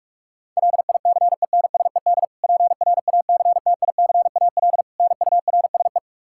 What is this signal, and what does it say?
Morse code, heard as a keyed tone, about 46 wpm, 700 Hz. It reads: LICENSED PRACTICAL NURSE
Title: Practice QSO - Exchange 3 - More Info - 45wpm